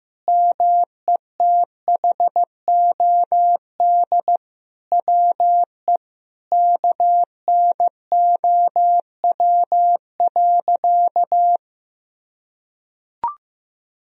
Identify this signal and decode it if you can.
Morse code, heard as a keyed tone, about 15 wpm, 700 Hz. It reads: METHOD WE KNOW.
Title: Sentences from Top 1000 Words - Review Part 3 - 15wpm